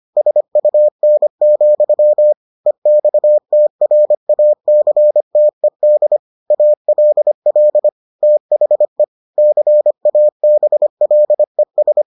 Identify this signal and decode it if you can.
Morse code, heard as a keyed tone, about 25 wpm, 600 Hz. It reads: SUN, EXTRACTED ALL THE CABLES